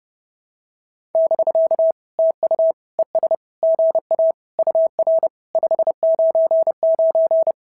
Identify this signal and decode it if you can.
Morse code, heard as a keyed tone, about 30 wpm, 650 Hz. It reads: <BK> TU ES GA UR 599
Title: SOTA QSO Practice 30wpm